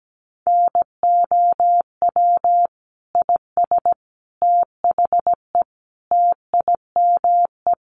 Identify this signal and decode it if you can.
Morse code, heard as a keyed tone, about 17 wpm, 700 Hz. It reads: NOW IS THE TIME
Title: Sets of 5 Words Encore2 17wpm